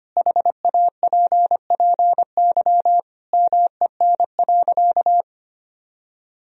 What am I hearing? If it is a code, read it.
Morse code, heard as a keyed tone, about 25 wpm, 700 Hz. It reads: HAPPY MEN.
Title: Sentences from Top 1000 Words - Review Part 1 - 25wpm